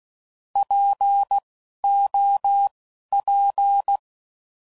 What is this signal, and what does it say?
Morse code, heard as a keyed tone, about 16 wpm, spaced as for 12 wpm, 800 Hz. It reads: POP
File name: Prefixes / By First Letter / Ps